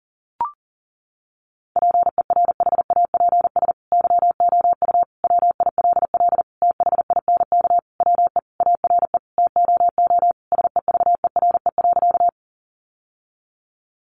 Morse code, heard as a keyed tone, about 40 wpm, 700 Hz. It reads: PERHAPS YOU WILL THINK WE ARE TOO SEVERE.
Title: Sentences from Top 1400 Words 40wpm